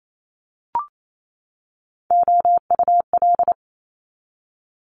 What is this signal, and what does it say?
Morse code, heard as a keyed tone, about 28 wpm, 700 Hz. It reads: OUL
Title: Multi-letter Phonemes 28wpm